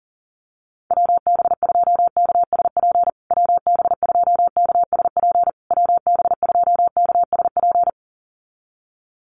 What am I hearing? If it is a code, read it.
Morse code, heard as a keyed tone, about 40 wpm, 700 Hz. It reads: WB2KSP WB2KSP WB2KSP
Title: USA Call Signs Repeated 3x 40wpm